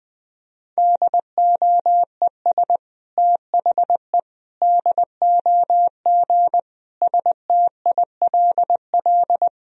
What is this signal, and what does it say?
Morse code, heard as a keyed tone, about 20 wpm, 700 Hz. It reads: DOES THE DOG STILL
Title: Sentences from Top 1000 Words - Review Part 3 - 20wpm